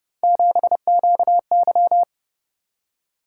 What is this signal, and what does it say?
Morse code, heard as a keyed tone, about 29 wpm, 700 Hz. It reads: 7QY